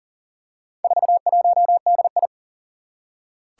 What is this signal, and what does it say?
Morse code, heard as a keyed tone, about 38 wpm, 700 Hz. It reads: 41DI
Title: Four Challenging Letter-Number Combinations 40wpm